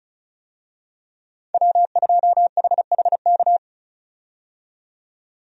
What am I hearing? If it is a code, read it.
Morse code, heard as a keyed tone, about 36 wpm, 700 Hz. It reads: W2HHK
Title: USA Call Signs - 5 Characters - 35wpm